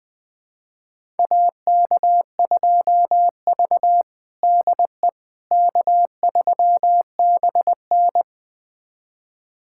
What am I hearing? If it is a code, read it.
Morse code, heard as a keyed tone, about 21 wpm, 700 Hz. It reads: AK2V DE K3BN
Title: USA Callsign DE Callsign - 4 Characters - 20wpm